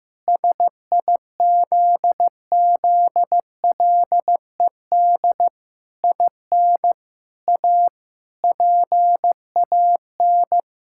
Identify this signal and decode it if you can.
Morse code, heard as a keyed tone, about 15 wpm, 700 Hz. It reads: SIZZLED IN A PAN
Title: Pangram Challenge v2 - 15wpm